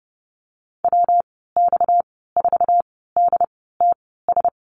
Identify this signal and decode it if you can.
Morse code, heard as a keyed tone, about 30 wpm, spaced as for 17 wpm, 700 Hz. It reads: WX4DTS
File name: Lesson 37g - Letter x - Call Signs 30wpm